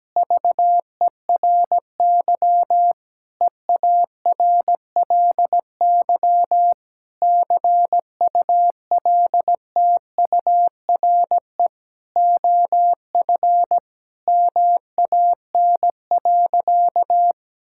Morse code, heard as a keyed tone, about 17 wpm, 700 Hz. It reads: VERY EARLY CULTURE OF MAN.
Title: Sentences from Top 1000 Words - Review Part 6 - 17wpm